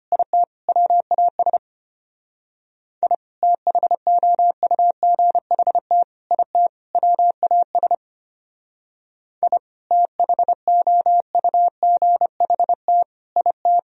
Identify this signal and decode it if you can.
Morse code, heard as a keyed tone, about 32 wpm, 700 Hz. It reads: IT WAS I THOUGHT IT WAS I THOUGHT IT